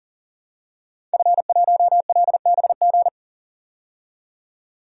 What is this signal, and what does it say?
Morse code, heard as a keyed tone, about 40 wpm, 700 Hz. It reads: F1LBG
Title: DXCC-100 Call Signs - 5 Characters - 40wpm